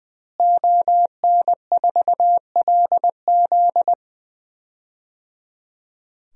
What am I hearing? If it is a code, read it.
Morse code, heard as a keyed tone, about 21 wpm, 700 Hz. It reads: ON4LZ
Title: DXCC-100 Call Signs - 5 Characters - Repeated 3x - 20wpm